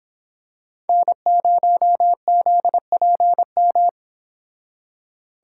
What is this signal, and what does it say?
Morse code, heard as a keyed tone, about 26 wpm, 700 Hz. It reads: N0ZPM
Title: Speed Racing - USA 1x3 Call Signs 22wpm